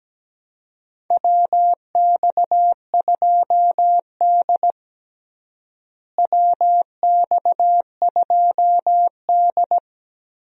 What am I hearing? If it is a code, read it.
Morse code, heard as a keyed tone, about 17 wpm, 700 Hz. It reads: WX2D WX2D
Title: USA Call Signs - 4 Characters - Repeated 2x - 17wpm